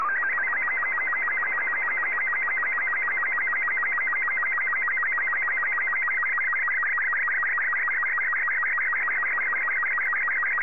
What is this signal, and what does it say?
SSTV (slow-scan television), Robot 36.